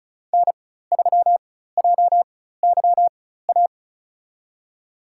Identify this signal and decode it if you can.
Morse code, heard as a keyed tone, about 34 wpm, spaced as for 16 wpm, 700 Hz. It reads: N3JYA
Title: USA Call Signs - 5 Characters - 4x Character Spacing - 35wpm